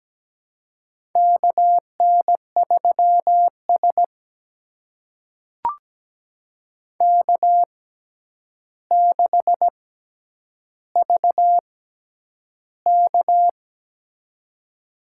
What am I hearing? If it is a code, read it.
Morse code, heard as a keyed tone, about 17 wpm, spaced as for 9 wpm, 700 Hz. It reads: KN3S K6VK
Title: USA Call Signs - 4 Characters - 6x Character Spacing - 17wpm